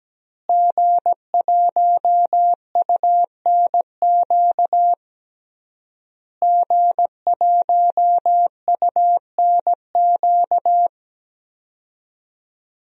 Morse code, heard as a keyed tone, about 17 wpm, 700 Hz. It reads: G1UNQ G1UNQ
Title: DXCC-100 Call Signs - 5 Characters - Repeated 3x - 17wpm